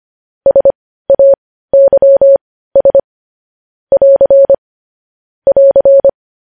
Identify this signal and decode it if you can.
Morse code, heard as a keyed tone, about 25 wpm, spaced as for 15 wpm, 550 Hz. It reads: SAYS <AR> <AR>
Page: News Headlines in Morse Code at 15 WPM